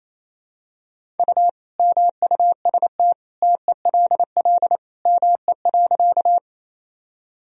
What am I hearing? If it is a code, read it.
Morse code, heard as a keyed tone, about 28 wpm, 700 Hz. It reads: U MUST TELL ME.
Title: Sentences from Top 200 Words and Abbreviations 28wpm